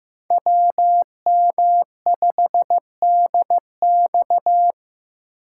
Morse code, heard as a keyed tone, about 15 wpm, 700 Hz. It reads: WM5DX
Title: K1USN SST QSO Practice - 15wpm